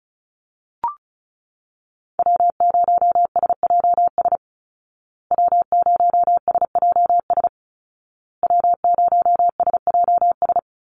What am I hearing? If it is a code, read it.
Morse code, heard as a keyed tone, about 34 wpm, 700 Hz. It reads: W0SJS W0SJS W0SJS
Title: Call Signs Repeated 2x - 35wpm